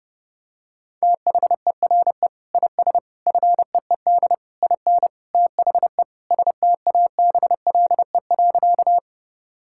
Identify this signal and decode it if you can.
Morse code, heard as a keyed tone, about 30 wpm, 700 Hz. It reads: THERE IS FEED IN THE STABLE.
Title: Sentences from Top 1600 Words 30wpm